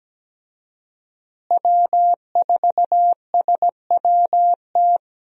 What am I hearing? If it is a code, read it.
Morse code, heard as a keyed tone, about 17 wpm, 700 Hz. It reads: W4SWT
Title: USA Call Signs - 5 Characters - Repeated 2x - 17wpm